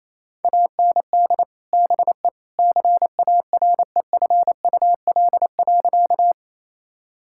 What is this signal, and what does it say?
Morse code, heard as a keyed tone, about 28 wpm, 700 Hz. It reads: AND BE CAREFUL.
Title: Sentences from Top 1600 Words 28wpm